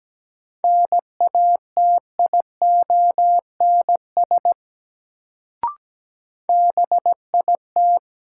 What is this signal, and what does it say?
Morse code, heard as a keyed tone, about 17 wpm, 700 Hz. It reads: NATIONS BIT
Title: Sets of 3 Words - New Word Drill - 17wpm